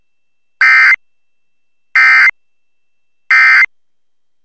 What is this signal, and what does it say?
Emergency Alert System (SAME) tones, heard as harsh buzzing bursts. It is heard from 0.6 to 3.6 s.